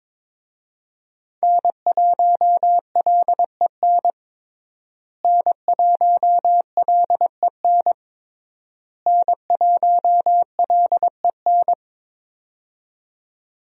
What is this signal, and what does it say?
Morse code, heard as a keyed tone, about 21 wpm, 700 Hz. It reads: N1LEN N1LEN N1LEN
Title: USA Call Signs - 5 Characters - Repeated 3x - 22wpm